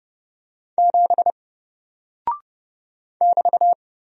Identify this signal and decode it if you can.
Morse code, heard as a keyed tone, about 30 wpm, 700 Hz. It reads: 7 <BT>
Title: Single Character (Letter:Number:Punctuation:Prosign) 30wpm